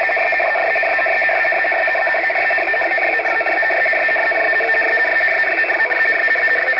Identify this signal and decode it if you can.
Two-tone teleprinter signal, about 450 Hz shift.
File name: UKR-RUS-radio-war- -1151utc-7055lsb-UKR-loop-RUS-jamming